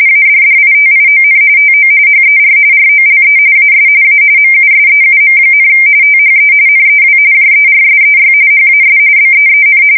RTTY, about 45.45 baud. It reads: ADDITION TO HER ROLE AS A MUSEUM, THE NEW JERSEY HAS BEEN
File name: fsk170 / battleshipnj